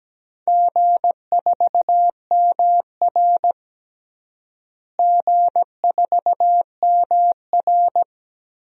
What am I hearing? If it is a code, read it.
Morse code, heard as a keyed tone, about 17 wpm, 700 Hz. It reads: G4MR G4MR
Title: DXCC-100 Call Signs - 4 Characters - Repeated 3x - 17wpm